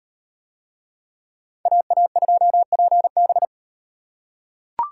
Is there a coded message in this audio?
Yes: Morse code from 1.7 to 3.5 s, heard as a keyed tone, about 35 wpm, 700 Hz. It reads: AA2PB